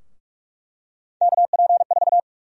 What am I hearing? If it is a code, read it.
Morse code, heard as a keyed tone, about 42 wpm, 700 Hz. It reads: KP4
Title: USA Call Sign Prefixes - Repeated 2x - 45wpm